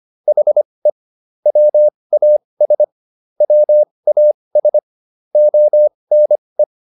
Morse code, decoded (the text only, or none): HE WAS WAS ONE